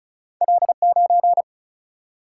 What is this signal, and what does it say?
Morse code, heard as a keyed tone, about 36 wpm, 700 Hz. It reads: L9
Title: Single-Letter Single-Number 35wpm